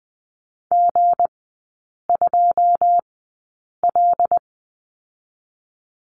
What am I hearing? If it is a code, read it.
Morse code, heard as a keyed tone, about 20 wpm, spaced as for 8 wpm, 700 Hz. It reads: G2L
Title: 205 Lesson 33e - Letter j - Three Characters 20wpm